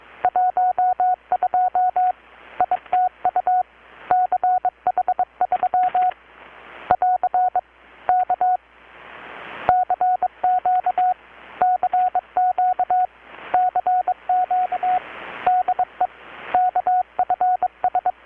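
Morse code, heard as a keyed tone, about 22 wpm, 720 Hz. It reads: 12 ITU CH3 <AR> K CQ CQ CQ DE KFS
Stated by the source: Morse from KPH's Night of Nights XX maritime radiotelegraphy event, received in Chiapas, Mexico about 3,800 km away on a Kenwood R-600 with a grounded 10 m long-wire antenna.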